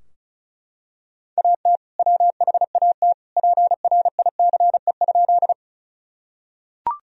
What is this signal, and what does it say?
Morse code, heard as a keyed tone, about 36 wpm, 700 Hz. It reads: AT WHAT PRICE?
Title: Sentences from Top 1000 Words - Review Part 4 - 35wpm